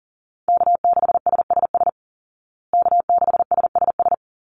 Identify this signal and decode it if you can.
Morse code, heard as a keyed tone, about 40 wpm, 700 Hz. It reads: K6SSS K6SSS
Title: Call Signs Repeated 3x - 40wpm